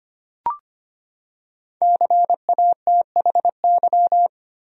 Morse code, heard as a keyed tone, about 25 wpm, 700 Hz. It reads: CATHY
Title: USA Common Names - Max 5 Letters - 25wpm